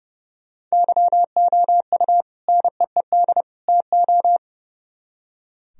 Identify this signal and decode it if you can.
Morse code, heard as a keyed tone, about 38 wpm, 700 Hz. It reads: YOU NEED TO